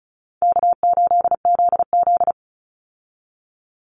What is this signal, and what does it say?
Morse code, heard as a keyed tone, about 34 wpm, 700 Hz. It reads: K8ZZ
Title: Call Signs Repeated 3x - 35wpm